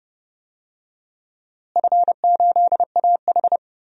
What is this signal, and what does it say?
Morse code, heard as a keyed tone, about 32 wpm, 700 Hz. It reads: F8AH
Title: DXCC-100 Call Signs - 4 Characters - Repeated 3x - 30wpm